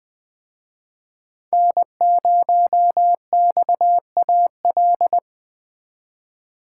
Morse code, heard as a keyed tone, about 21 wpm, 700 Hz. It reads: N0XAL
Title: Speed Racing - USA 1x3 Call Signs 17wpm